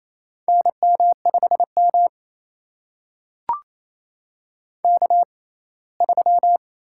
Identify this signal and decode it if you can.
Morse code, heard as a keyed tone, about 28 wpm, spaced as for 19 wpm, 700 Hz. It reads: NM5M K3
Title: USA Call Signs - 6x Character Spacing - 28wpm